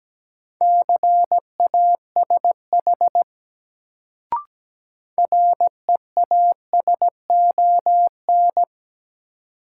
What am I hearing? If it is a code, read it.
Morse code, heard as a keyed tone, about 17 wpm, 700 Hz. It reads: CASH REASON
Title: Sentences from Top 700 Words - New Word Drill - 17wpm